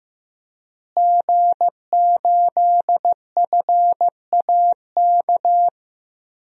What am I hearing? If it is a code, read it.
Morse code, heard as a keyed tone, about 15 wpm, 700 Hz. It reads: G8FAK
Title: DXCC-100 Call Signs - 5 Characters - Repeated 2x - 15wpm